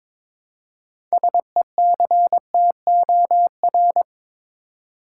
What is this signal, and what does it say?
Morse code, heard as a keyed tone, about 21 wpm, 700 Hz. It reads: SECTOR
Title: Six Letter Words - 22wpm